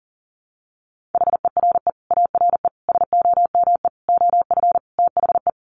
Morse code, heard as a keyed tone, about 40 wpm, 700 Hz. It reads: HERE ARE SOME OF THE
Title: Sets of 5 Words Encore 40wpm